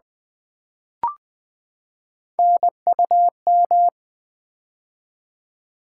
Morse code, heard as a keyed tone, about 21 wpm, 700 Hz. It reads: NUM
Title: Build A Word - 3 Letters At A Time - 8-Letter Words - 20wpm